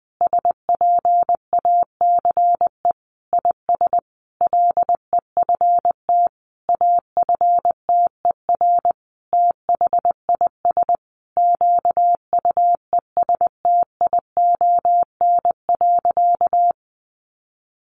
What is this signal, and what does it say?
Morse code, heard as a keyed tone, about 20 wpm, 700 Hz. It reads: SPACE IS LEFT AFTER THIS QUESTION.
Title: Sentences from Top 1500 Words 20wpm